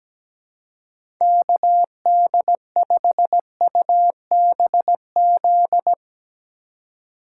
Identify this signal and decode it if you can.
Morse code, heard as a keyed tone, about 17 wpm, 700 Hz. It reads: KD5UBZ
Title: USA Call Signs - 6 Characters - 2x Character Spacing - 17wpm